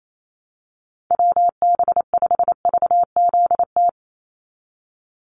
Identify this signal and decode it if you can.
Morse code, heard as a keyed tone, about 28 wpm, 700 Hz. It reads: WB5VZT
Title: USA Call Signs 28wpm